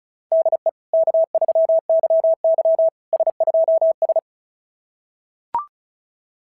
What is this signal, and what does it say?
Morse code, heard as a keyed tone, about 36 wpm, 690 Hz. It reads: DE K3YY S2S
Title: SOTA QSO Practice 35wpm